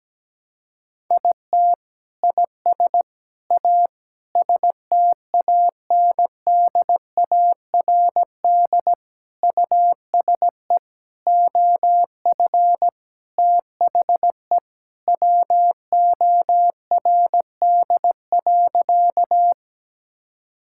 Morse code, heard as a keyed tone, about 17 wpm, 700 Hz. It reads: IT IS A STANDARD USE OF THE WORD.
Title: Sentences from Top 1000 Words - Review Part 6 - 17wpm